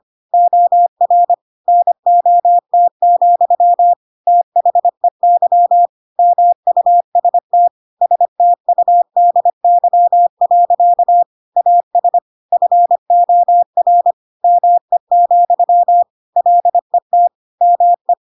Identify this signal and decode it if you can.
Morse code, heard as a keyed tone, about 25 wpm, 700 Hz. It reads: OR NOT, THEY MUST STUDY. AS FOR ME, LET ME
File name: Pinocchio / Pinocchio ch4 25 25 ik1hge